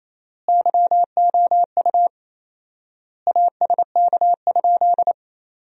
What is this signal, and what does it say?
Morse code, heard as a keyed tone, about 28 wpm, 700 Hz. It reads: YOU ASK?
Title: 4x Word-Spacing - Sentences from Top 200 Words - 28wpm